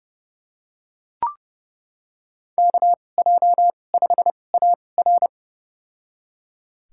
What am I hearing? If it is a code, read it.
Morse code, heard as a keyed tone, about 31 wpm, spaced as for 22 wpm, 700 Hz. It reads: KJ5AR
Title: USA Call Signs - 5 Characters - 2x Character Spacing - 30wpm